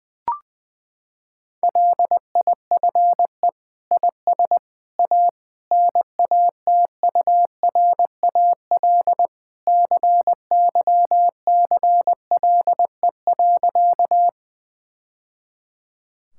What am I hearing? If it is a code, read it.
Morse code, heard as a keyed tone, about 21 wpm, 700 Hz. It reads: LIFE IS A NATURAL CYCLE.
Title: Sentences from Top 1000 Words - Review Part 6 - 20wpm